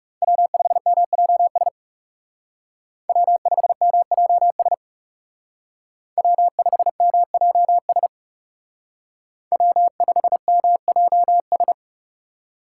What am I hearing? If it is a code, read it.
Morse code, heard as a keyed tone, about 35 wpm, 700 Hz. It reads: W5MJS W5MJS W5MJS W5MJS
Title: Speed Racing - USA 1x3 Call Signs 30wpm